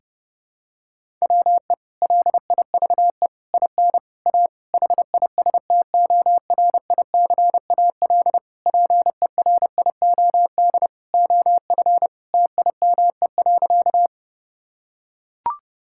Morse code, heard as a keyed tone, about 32 wpm, 700 Hz. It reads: WE LIVE IN A HISTORICAL PERIOD OF TIME.
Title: Sentences from Top 1000 Words - Review Part 5 - 30wpm